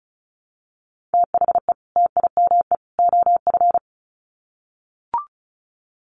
Morse code, heard as a keyed tone, about 34 wpm, 700 Hz. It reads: THE TIME OF
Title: Sets of 3 Words 35wpm